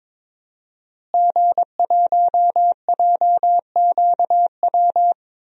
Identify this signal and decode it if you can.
Morse code, heard as a keyed tone, about 22 wpm, 700 Hz. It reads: G1JQW
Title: DXCC-100 Call Signs - 5 Characters - Repeated 3x - 22wpm